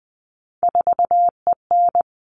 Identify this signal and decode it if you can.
Morse code, heard as a keyed tone, about 20 wpm, 700 Hz. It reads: 4EN
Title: All Cut Numbers - 3-Digit 20wpm